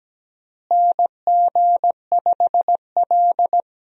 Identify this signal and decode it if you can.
Morse code, heard as a keyed tone, about 17 wpm, 700 Hz. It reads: NG5L
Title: USA Call Signs - 4 Characters - Repeated 2x - 17wpm